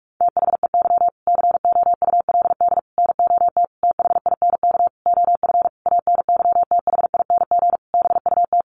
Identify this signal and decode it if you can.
Morse code, heard as a keyed tone, about 44 wpm, 700 Hz. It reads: THEY COULD NOT THINK OF ANYTHING BUT
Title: Sentences from Top 1000 Words 45wpm